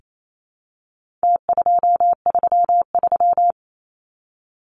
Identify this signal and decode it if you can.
Morse code, heard as a keyed tone, about 28 wpm, 700 Hz. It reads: T233
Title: Numbers - Contest Style - 4-Digit 28wpm